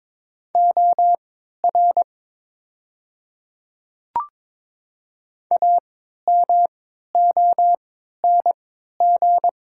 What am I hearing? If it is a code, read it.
Morse code, heard as a keyed tone, about 22 wpm, spaced as for 13 wpm, 700 Hz. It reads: OR AMONG
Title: 3x Character Spacing - Top 500 Words - 22wpm